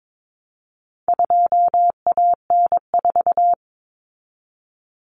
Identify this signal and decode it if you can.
Morse code, heard as a keyed tone, about 22 wpm, 700 Hz. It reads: 2AN4
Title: Common Cut Numbers - 4-Digit 22wpm